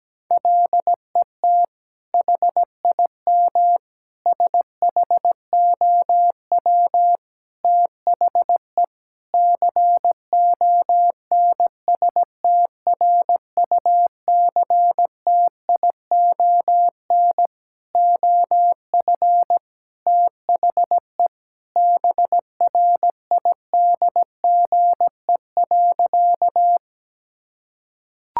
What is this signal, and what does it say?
Morse code, heard as a keyed tone, about 17 wpm, 700 Hz. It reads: LET HIM SHOW THE CONSTRUCTION OF THE BRIDGE.
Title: Sentences from Top 1600 Words 17wpm